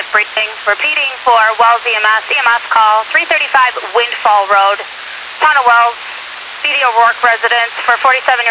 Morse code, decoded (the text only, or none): none